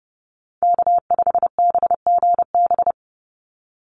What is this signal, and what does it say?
Morse code, heard as a keyed tone, about 30 wpm, 700 Hz. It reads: K5BGB
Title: Call Signs - 30wpm